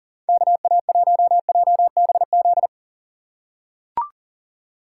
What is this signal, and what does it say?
Morse code, heard as a keyed tone, about 38 wpm, 700 Hz. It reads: KA1JBZ
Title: USA Call Signs - 6 Characters - 4x Character Spacing - 40wpm